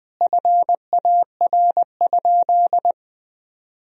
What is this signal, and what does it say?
Morse code, heard as a keyed tone, about 20 wpm, 700 Hz. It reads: FAR?
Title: Sentences from Top 300 Words and Abbreviations 20wpm